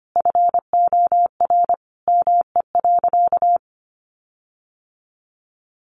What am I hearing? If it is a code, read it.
Morse code, heard as a keyed tone, about 25 wpm, 700 Hz. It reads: FOR ME.
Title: Sentences from Top 1500 Words 25wpm